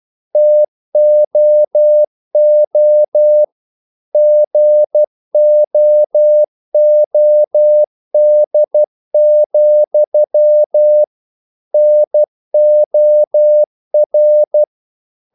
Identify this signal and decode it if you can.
Morse code, heard as a keyed tone, about 12 wpm, 600 Hz. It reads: TOO GOOD, NOR
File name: CW / Practice / If12